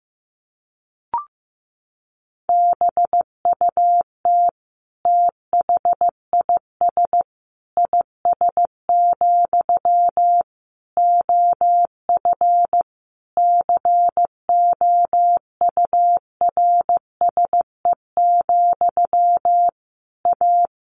Morse code, heard as a keyed tone, about 15 wpm, 700 Hz. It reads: BUT THIS IS, OF COURSE, A
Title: Sentences from Top 1500 Words 15wpm